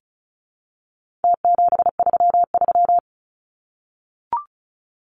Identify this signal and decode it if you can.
Morse code, heard as a keyed tone, about 34 wpm, 700 Hz. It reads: T733
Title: Numbers - Contest Style - 4-Digit 35wpm